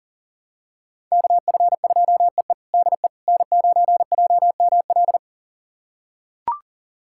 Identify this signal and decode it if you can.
Morse code, heard as a keyed tone, about 40 wpm, 700 Hz. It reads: KF2EE DE N9JML
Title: USA Callsign DE Callsign - 5 Characters - 40wpm